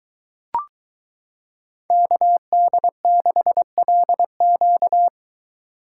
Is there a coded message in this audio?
Yes: Morse code from 1.9 to 5.1 s, heard as a keyed tone, about 22 wpm, 700 Hz. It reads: KD6LQ